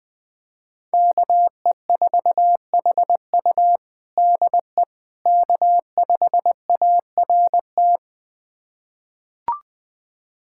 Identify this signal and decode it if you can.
Morse code, heard as a keyed tone, about 20 wpm, 700 Hz. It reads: KE4HU DE K5ART
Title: USA Callsign DE Callsign - 5 Characters - 20wpm